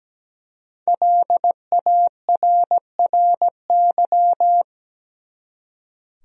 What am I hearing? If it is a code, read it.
Morse code, heard as a keyed tone, about 17 wpm, 700 Hz. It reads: LARRY
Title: USA Common Names - 17wpm